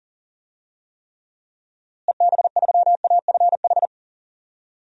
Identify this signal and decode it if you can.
Morse code, heard as a keyed tone, about 40 wpm, 700 Hz. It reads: EB3AFH
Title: DXCC-100 Call Signs - 6 Characters - Repeated 2x - 40wpm